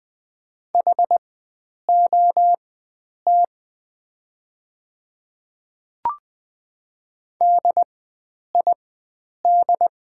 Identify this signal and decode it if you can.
Morse code, heard as a keyed tone, about 20 wpm, spaced as for 9 wpm, 700 Hz. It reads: HOT DID
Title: 4x Character Spacing - Top 100 Words - 20wpm